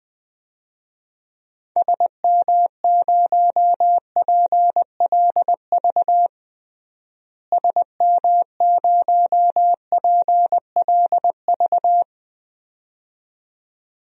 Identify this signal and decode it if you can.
Morse code, heard as a keyed tone, about 20 wpm, 700 Hz. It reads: SM0PLV SM0PLV
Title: DXCC-100 Call Signs - 6 Characters - Repeated 2x - 20wpm